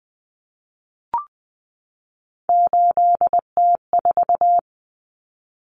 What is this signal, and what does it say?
Morse code, heard as a keyed tone, about 20 wpm, 700 Hz. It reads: 8T4
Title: Common Cut Numbers - 3-Digit 20wpm